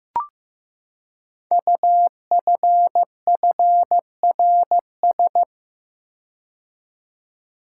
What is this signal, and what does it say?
Morse code, heard as a keyed tone, about 15 wpm, 700 Hz. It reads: UFFRS
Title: Code Groups - 5 Letters - 15wpm